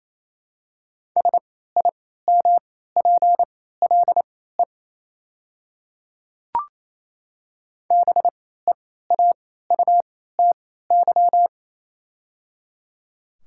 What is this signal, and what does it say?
Morse code, heard as a keyed tone, about 28 wpm, spaced as for 16 wpm, 700 Hz. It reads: SIMPLE BEAUTY
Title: 3x Character Spacing - Top 500 Words - 28wpm